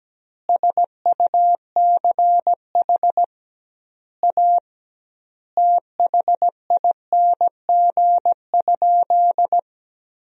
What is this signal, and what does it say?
Morse code, heard as a keyed tone, about 17 wpm, 700 Hz. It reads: SUCH A THING?
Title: 2x Word-Spacing - Sentences from Top 200 Words - 17wpm